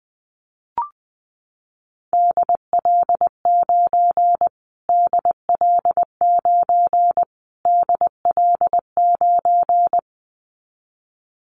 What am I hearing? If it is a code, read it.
Morse code, heard as a keyed tone, about 20 wpm, 700 Hz. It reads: DL9 DL9 DL9
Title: Callsign Prefixes for Top 25 DXCC - Repeated 3x - 20wpm